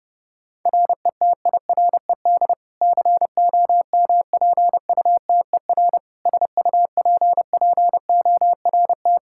Morse code, heard as a keyed tone, about 30 wpm, 700 Hz. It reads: RETIRED COMPUTER SUPPORT
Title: Practice QSO - Exchange 3 - More Info - 30wpm